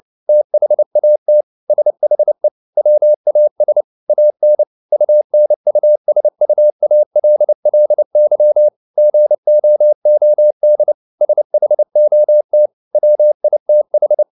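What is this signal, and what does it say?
Morse code, heard as a keyed tone, about 29 wpm, 600 Hz. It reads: THAT SHE WAS AN UNUSUALLY GOOD SHOT WITH
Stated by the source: Morse code at 29 wpm, sending Ken Follett's novel Never in English, from CW med Gnister.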